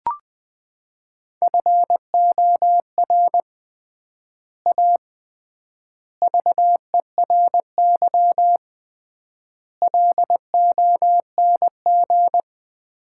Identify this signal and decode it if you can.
Morse code, heard as a keyed tone, about 21 wpm, 700 Hz. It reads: FOR A VERY LONG
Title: 3x Word-Spacing - Sets of 5 Words - 20wpm